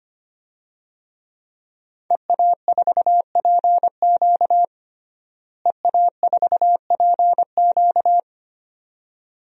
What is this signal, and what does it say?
Morse code, heard as a keyed tone, about 25 wpm, 700 Hz. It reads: EA4PQ EA4PQ
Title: DXCC-100 Call Signs - 5 Characters - Repeated 2x - 25wpm